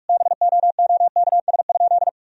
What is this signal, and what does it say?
Morse code, heard as a keyed tone, about 46 wpm, 700 Hz. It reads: BOOKS?
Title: Sentences from Top 1000 Words - Review Part 4 - 45wpm